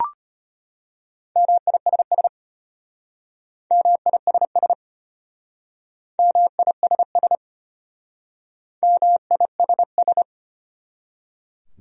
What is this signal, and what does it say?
Morse code, heard as a keyed tone, about 33 wpm, 700 Hz. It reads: MISS MISS MISS MISS